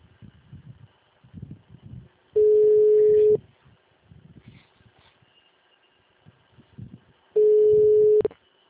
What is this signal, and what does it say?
Ringback tone (Europe, 425 Hz), with speech.